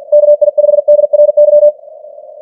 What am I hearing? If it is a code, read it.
Morse code, heard as a keyed tone, about 46 wpm, 590 Hz. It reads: KI5DRY
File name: archived / CWBros